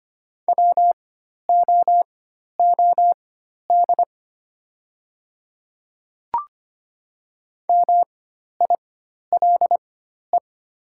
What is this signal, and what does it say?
Morse code, heard as a keyed tone, about 25 wpm, spaced as for 12 wpm, 700 Hz. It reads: WOOD MILE
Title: 4x Character Spacing - Top 300 Words - 25wpm